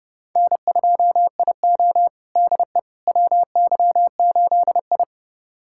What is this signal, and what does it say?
Morse code, heard as a keyed tone, about 32 wpm, 700 Hz. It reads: N2IO DE WY8I
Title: USA Callsign DE Callsign - 4 Characters - 30wpm